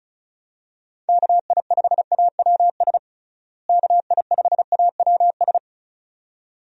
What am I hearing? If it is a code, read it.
Morse code, heard as a keyed tone, about 36 wpm, 700 Hz. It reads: KI5AWS KI5AWS
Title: USA Call Signs - 6 Characters - Repeated 2x - 35wpm